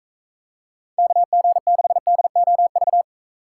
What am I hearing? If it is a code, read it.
Morse code, heard as a keyed tone, about 40 wpm, 700 Hz. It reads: KG6DOV